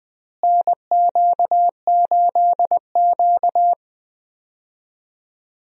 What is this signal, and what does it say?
Morse code, heard as a keyed tone, about 21 wpm, 700 Hz. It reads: NQ8Q
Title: USA Call Signs - 4 Characters - Repeated 3x - 20wpm